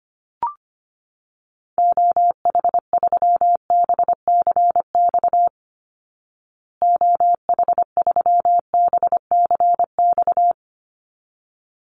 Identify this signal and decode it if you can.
Morse code, heard as a keyed tone, about 25 wpm, 700 Hz. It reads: OH3BCX OH3BCX
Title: Call Signs Repeated 3x - 25wpm